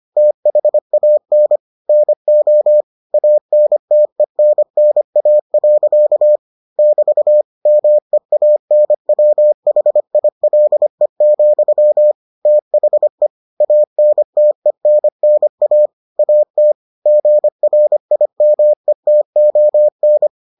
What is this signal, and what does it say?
Morse code, heard as a keyed tone, about 25 wpm, 600 Hz. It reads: THAN NO ANTENNA. <BT> MEANWHILE, THE ANTENNA AT GRIMETON